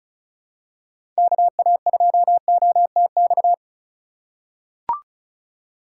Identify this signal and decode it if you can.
Morse code, heard as a keyed tone, about 36 wpm, 700 Hz. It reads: KA2OTX
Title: USA Call Signs - 6 Characters - 2x Character Spacing - 35wpm